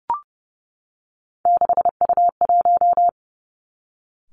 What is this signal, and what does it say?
Morse code, heard as a keyed tone, about 30 wpm, 700 Hz. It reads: 6U1
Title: Three Challenging Letter-Number Combinations 30wpm